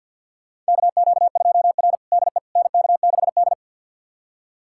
Morse code, heard as a keyed tone, about 52 wpm, 700 Hz. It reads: KQ2R DE NK6D
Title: USA Callsign DE Callsign - 4 Characters - 50wpm